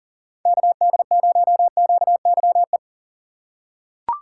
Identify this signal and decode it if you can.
Morse code, heard as a keyed tone, about 37 wpm, 700 Hz. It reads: KD0QYE
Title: USA Call Signs - 6 Characters - 4x Character Spacing - 40wpm